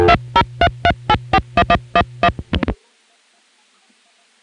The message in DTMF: 2033841111